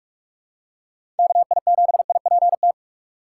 Morse code, heard as a keyed tone, about 43 wpm, 700 Hz. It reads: KI7IPT
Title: USA Call Signs - 6 Characters - 4x Character Spacing - 45wpm